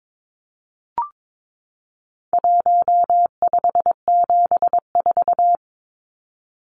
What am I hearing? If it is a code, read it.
Morse code, heard as a keyed tone, about 22 wpm, 700 Hz. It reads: 1574
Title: Numbers - 4-Digit 22wpm